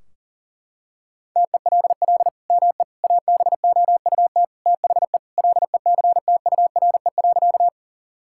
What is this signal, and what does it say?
Morse code, heard as a keyed tone, about 40 wpm, 700 Hz. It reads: TELL ME ABOUT THE LECTURE.
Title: Sentences from Top 1600 Words 40wpm